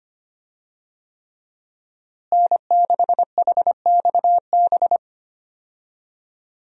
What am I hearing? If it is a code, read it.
Morse code, heard as a keyed tone, about 25 wpm, 700 Hz. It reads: N6HXB
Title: USA Call Signs - 5 Characters - Repeated 3x - 25wpm